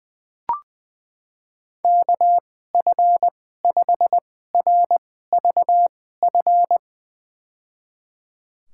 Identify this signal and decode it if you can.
Morse code, heard as a keyed tone, about 20 wpm, spaced as for 14 wpm, 700 Hz. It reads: KF5RVF
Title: USA Call Signs - 6 Characters - 2x Character Spacing - 20wpm